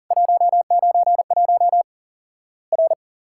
Morse code, heard as a keyed tone, about 40 wpm, 700 Hz. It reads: 191 R
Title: Worked All Europe DX Contest - QTC Exchange - 40wpm